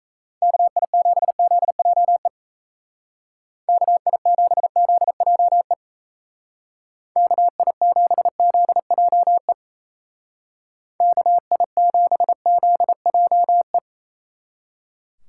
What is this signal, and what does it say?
Morse code, heard as a keyed tone, about 37 wpm, 700 Hz. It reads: KI7ZJE KI7ZJE KI7ZJE KI7ZJE